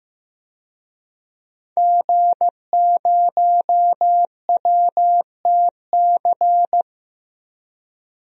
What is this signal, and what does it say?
Morse code, heard as a keyed tone, about 15 wpm, 700 Hz. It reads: G0WTC
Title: DXCC-100 Call Signs - 5 Characters - Repeated 2x - 15wpm